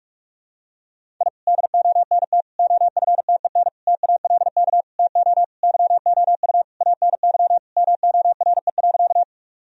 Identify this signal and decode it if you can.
Morse code, heard as a keyed tone, about 46 wpm, 700 Hz. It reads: I DONT OFTEN TALK TO YOU ANY MORE.
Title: Sentences from Top 1000 Words - Review Part 1 - 45wpm